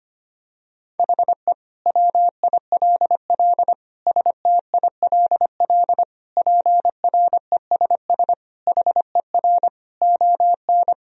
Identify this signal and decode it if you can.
Morse code, heard as a keyed tone, about 25 wpm, 700 Hz. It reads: HE WILL STILL PRESS HER ON
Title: Sentences from Top 1000 Words - Review Part 1 - 25wpm